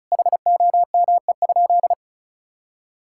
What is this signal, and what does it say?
Morse code, heard as a keyed tone, about 36 wpm, 700 Hz. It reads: HOME?
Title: Sentences from Top 1000 Words - Review Part 6 - 35wpm